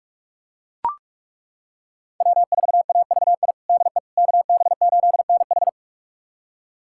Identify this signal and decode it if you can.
Morse code, heard as a keyed tone, about 42 wpm, 700 Hz. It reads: W4AVI DE KB8NH
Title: USA Callsign DE Callsign - 5 Characters - 45wpm